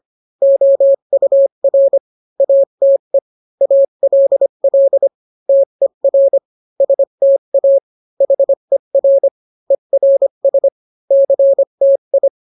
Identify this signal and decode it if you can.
Morse code, heard as a keyed tone, about 25 wpm, 550 Hz. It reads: OUR ATE ALL TER STA HER ERS CTI